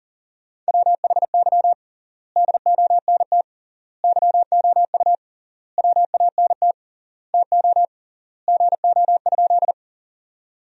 Morse code, heard as a keyed tone, about 40 wpm, 700 Hz. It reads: WHY DONT YOU WANT TO GO?
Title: 3x Word-Spacing - Sentences from Top 300 Words - 40wpm